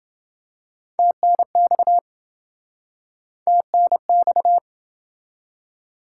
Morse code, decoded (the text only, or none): TNX TNX